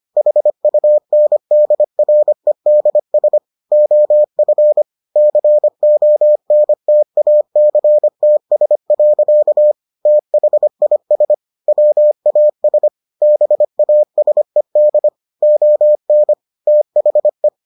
Morse code, heard as a keyed tone, about 25 wpm, 600 Hz. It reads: HUNDREDS OF CONTACTS. THIS WAS BASED ON THE